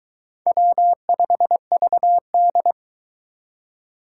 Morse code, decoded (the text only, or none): W5VD